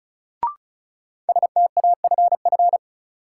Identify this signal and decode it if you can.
Morse code, heard as a keyed tone, about 36 wpm, 700 Hz. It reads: STAFF
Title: Sentences from Top 1200 Words - New Word Drill Alt - 35wpm